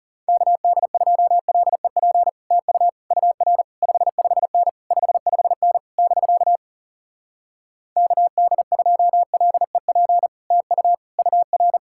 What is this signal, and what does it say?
Morse code, heard as a keyed tone, about 40 wpm, 700 Hz. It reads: KD2LEP TU UR 55N 55N <BK> KD2LEP TU UR